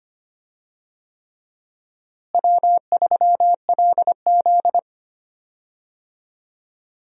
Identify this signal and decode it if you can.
Morse code, heard as a keyed tone, about 25 wpm, 700 Hz. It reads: W3LZ
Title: USA Call Signs - 4 Characters - Repeated 3x - 25wpm